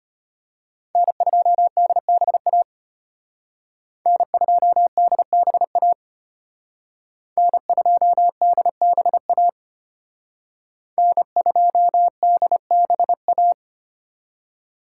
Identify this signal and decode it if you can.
Morse code, heard as a keyed tone, about 32 wpm, 700 Hz. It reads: N2DBA N2DBA N2DBA N2DBA